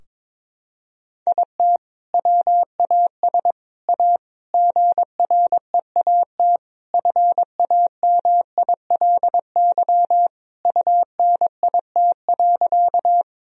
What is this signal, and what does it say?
Morse code, heard as a keyed tone, about 22 wpm, 700 Hz. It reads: IT WAS A GREAT FAMILY UNIT.
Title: Sentences from Top 1000 Words - Review Part 2 - 22wpm